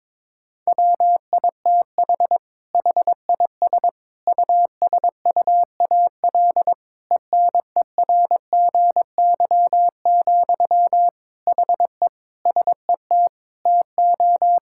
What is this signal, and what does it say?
Morse code, heard as a keyed tone, about 21 wpm, 700 Hz. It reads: WITH HIS USUAL ENERGY, HE SET TO
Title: Sentences from Top 1000 Words - Review Part 6 - 22wpm